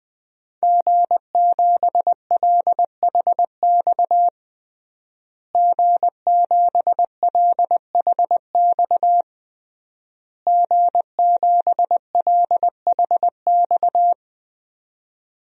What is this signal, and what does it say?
Morse code, heard as a keyed tone, about 20 wpm, 700 Hz. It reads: G7LHX G7LHX G7LHX
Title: DXCC-100 Call Signs - 5 Characters - Repeated 3x - 20wpm